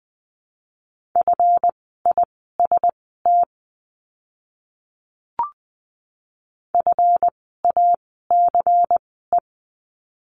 Morse code, heard as a keyed tone, about 20 wpm, spaced as for 14 wpm, 700 Hz. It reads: FIST FACE
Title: 139 Lesson 23f - Letter f - Words 20wpm